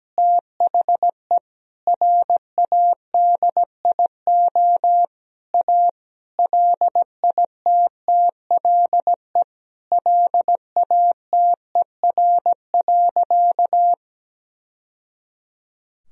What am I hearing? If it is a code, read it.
Morse code, heard as a keyed tone, about 17 wpm, 700 Hz. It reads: THE RADIO A LITTLE LATER.
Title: Sentences from Top 1000 Words - Review Part 3 - 17wpm